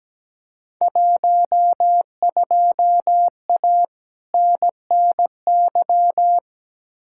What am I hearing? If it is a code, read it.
Morse code, heard as a keyed tone, about 17 wpm, 700 Hz. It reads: 12A NNY
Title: Field Day Exchanges Encore - 17wpm